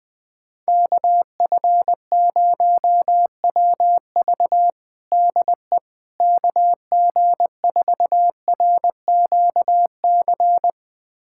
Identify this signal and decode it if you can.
Morse code, heard as a keyed tone, about 20 wpm, 700 Hz. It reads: KF0WV DE KG4RQC
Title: USA Callsign DE Callsign 20wpm